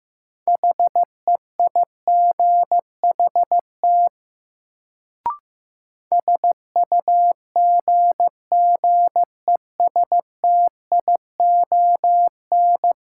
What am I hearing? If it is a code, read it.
Morse code, heard as a keyed tone, about 15 wpm, 700 Hz. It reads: HEIGHT SUGGESTION
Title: Sentences from Top 1600 Words - New Word Drill Alt - 15wpm